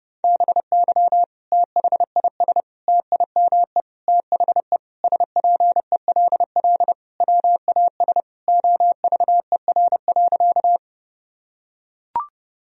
Morse code, heard as a keyed tone, about 30 wpm, 700 Hz. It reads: BY THIS TIME THE SPELL WAS OVER.
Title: Sentences from Top 1000 Words - Review Part 1 - 30wpm